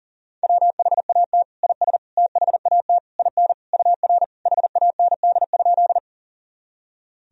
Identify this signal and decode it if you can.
Morse code, heard as a keyed tone, about 40 wpm, 700 Hz. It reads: WHAT IS THAT IN UR HAND?
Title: Sentences from Top 200 Words and Abbreviations 40wpm